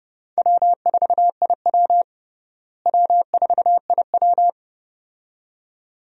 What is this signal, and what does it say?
Morse code, heard as a keyed tone, about 32 wpm, 700 Hz. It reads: W4IW W4IW
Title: USA Call Signs - 4 Characters - Repeated 2x - 30wpm